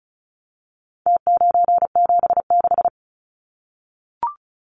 Morse code, heard as a keyed tone, about 34 wpm, 700 Hz. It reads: T976
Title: Numbers - Contest Style - 4-Digit 35wpm